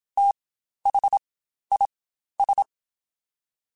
Morse code, heard as a keyed tone, about 27 wpm, spaced as for 12 wpm, 800 Hz. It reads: THIS
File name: Spaced Characters / Spaced Characters-mp3s / Weather4 24 sp @25